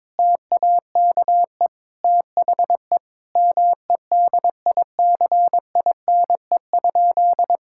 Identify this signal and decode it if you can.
Morse code, heard as a keyed tone, about 21 wpm, 700 Hz. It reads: TAKE THE MEDICINE?
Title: Sentences from Top 1000 Words - Review Part 6 - 22wpm